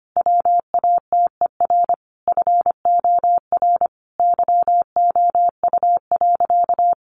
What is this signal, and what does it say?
Morse code, heard as a keyed tone, about 25 wpm, 700 Hz. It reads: WATER FOR YOU.
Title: Sentences from Top 1300 Words 25wpm